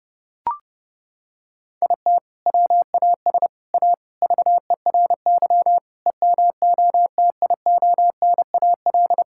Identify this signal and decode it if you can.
Morse code, heard as a keyed tone, about 30 wpm, 700 Hz. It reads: IT WAS A VERY EMOTIONAL
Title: Sentences from Top 1000 Words - Review Part 5 - 30wpm